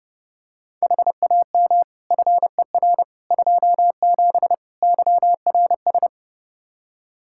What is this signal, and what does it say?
Morse code, heard as a keyed tone, about 32 wpm, 700 Hz. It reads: HAM FER 27 YRS
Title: Practice QSO - Exchange 3 - More Info - 30wpm